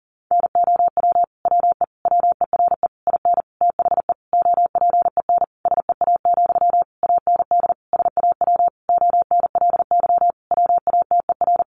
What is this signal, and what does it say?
Morse code, heard as a keyed tone, about 40 wpm, 700 Hz. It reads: NOW WE WERE IN THE OPEN SEA, AND SAW ONLY WATER
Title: Sentences from Top 1200 Words 40wpm